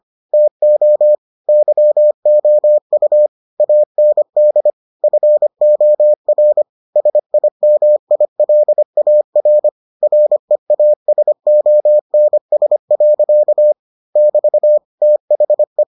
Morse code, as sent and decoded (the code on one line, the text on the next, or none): - --- / -.-- --- ..- / .- -. -.. / ..-. --- .-. / ... .. -- .. .-.. .- .-. / .-. .- ... --- -. ... .-.-.- / -...- / - .... .
TO YOU AND FOR SIMILAR REASONS. <BT> THE